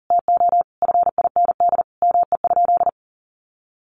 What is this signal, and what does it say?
Morse code, heard as a keyed tone, about 37 wpm, 700 Hz. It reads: TO FIND ME?
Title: Sentences from Top 1400 Words 40wpm